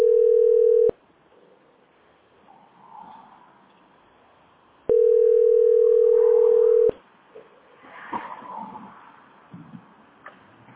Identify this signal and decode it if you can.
Ringback tone (North America), 440 + 480 Hz.